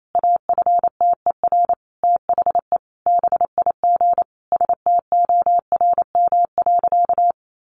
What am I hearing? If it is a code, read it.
Morse code, heard as a keyed tone, about 28 wpm, 700 Hz. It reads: AFTER THE BIG STORM.
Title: Sentences from Top 1500 Words 28wpm